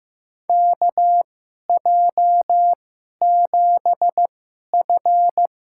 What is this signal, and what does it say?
Morse code, heard as a keyed tone, about 15 wpm, spaced as for 11 wpm, 700 Hz. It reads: KJ7F
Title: USA Call Signs - 6 Characters - 2x Character Spacing - 15wpm